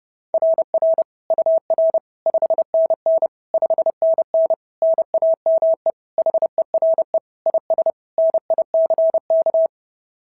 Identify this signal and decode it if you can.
Morse code, heard as a keyed tone, about 30 wpm, 690 Hz. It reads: RR UR 5NN 5NN NAME HERE IS NICK
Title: SOTA QSO Practice 30wpm